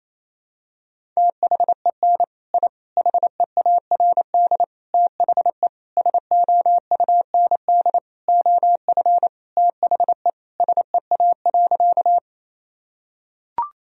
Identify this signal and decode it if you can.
Morse code, heard as a keyed tone, about 28 wpm, 700 Hz. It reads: THEN I HEARD THE SOUND OF THE SEA.
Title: Sentences from Top 1000 Words - Review Part 1 - 28wpm